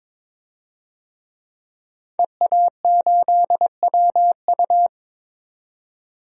Morse code, heard as a keyed tone, about 22 wpm, 700 Hz. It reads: EA8WU
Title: DXCC-100 Call Signs - 5 Characters - 22wpm